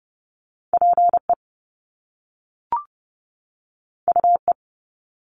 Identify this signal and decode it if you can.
Morse code, heard as a keyed tone, about 30 wpm, 700 Hz. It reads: PE UE
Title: Top Two Letter Combinations in Words 30wpm